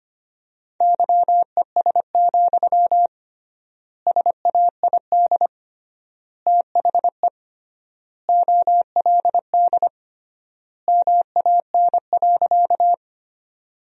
Morse code, heard as a keyed tone, about 25 wpm, 700 Hz. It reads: YES, SAID THE OLD MAN.
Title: 3x Word-Spacing - Sentences from Top 500 Words - 25wpm